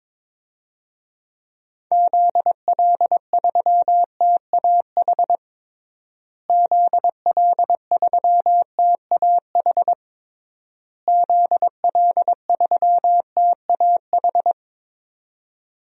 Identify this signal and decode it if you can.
Morse code, heard as a keyed tone, about 22 wpm, 700 Hz. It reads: ZL3TAH ZL3TAH ZL3TAH
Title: DXCC-100 Call Signs - 6 Characters - Repeated 3x - 22wpm